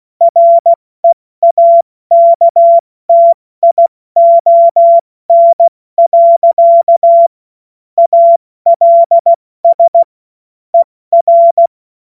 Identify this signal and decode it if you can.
Morse code, heard as a keyed tone, about 16 wpm, spaced as for 14 wpm, 680 Hz. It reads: REAKTION. ALS ER
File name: Morsekurs / Witze / Witz 05. April 2022 70 F80